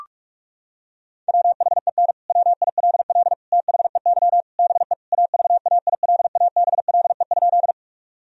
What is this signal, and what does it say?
Morse code, heard as a keyed tone, about 41 wpm, 700 Hz. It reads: WHEN WILL THEY BE AVAILABLE?
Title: Sentences from Top 1000 Words - Review Part 3 - 45wpm